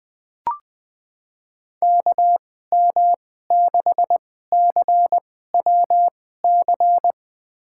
Morse code, heard as a keyed tone, about 20 wpm, spaced as for 15 wpm, 700 Hz. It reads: KM6CWC
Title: USA Call Signs - 6 Characters - 2x Character Spacing - 20wpm